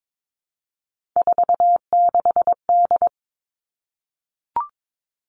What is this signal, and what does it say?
Morse code, heard as a keyed tone, about 22 wpm, 700 Hz. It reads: 46D
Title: Three Challenging Letter-Number Combinations 22wpm